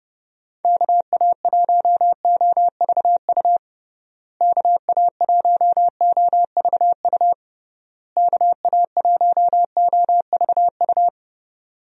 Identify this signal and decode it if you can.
Morse code, heard as a keyed tone, about 30 wpm, 700 Hz. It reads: KA1OVU KA1OVU KA1OVU
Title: USA Call Signs - 6 Characters - Repeated 3x - 30wpm